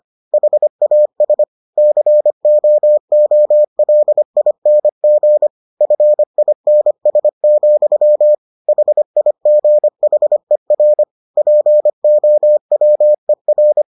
Morse code, heard as a keyed tone, about 25 wpm, 600 Hz. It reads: HAS COOLING FINS, HIGHER POWER